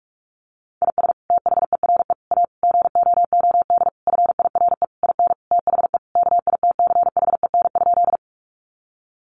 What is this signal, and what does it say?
Morse code, heard as a keyed tone, about 44 wpm, 700 Hz. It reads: IS THERE A GOOD FIRE IN THE KITCHEN?
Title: Sentences from Top 1400 Words 45wpm